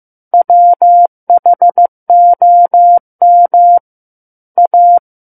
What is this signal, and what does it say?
Morse code, heard as a keyed tone, about 15 wpm, 700 Hz. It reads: WHOM A
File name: Chapter 6 - 15 WPM